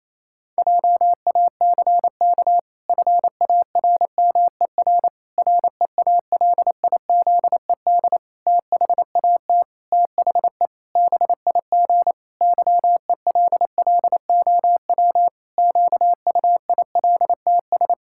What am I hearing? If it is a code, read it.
Morse code, heard as a keyed tone, about 27 wpm, 700 Hz. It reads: JACK FARMER REALIZED THAT THE BIG YELLOW QUILTS
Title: Pangram Challenge v2 - 28wpm